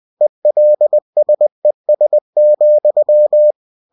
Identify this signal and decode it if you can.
Morse code, heard as a keyed tone, about 20 wpm, 600 Hz. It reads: ELSES,
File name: cw-books / winnie-the-pooh / winnie-20wpm-page-0028